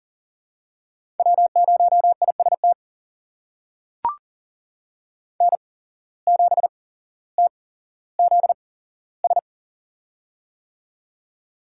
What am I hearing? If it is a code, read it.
Morse code, heard as a keyed tone, about 40 wpm, spaced as for 17 wpm, 700 Hz. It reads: W0IST N7TZS
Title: USA Call Signs - 5 Characters - 8x Character Spacing - 40wpm